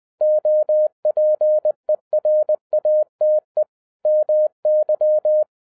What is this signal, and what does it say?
Morse code, heard as a keyed tone, about 20 wpm, 610 Hz. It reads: OPERATE MY
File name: 2011-07-mp3 / n58 BACK FROM A BIG ACCIDENT 20wpm